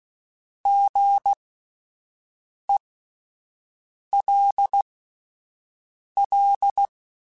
Morse code, heard as a keyed tone, about 16 wpm, spaced as for 5 wpm, 800 Hz. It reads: GELL
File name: Alphabetical / Alphabetical mp3s / Gs @5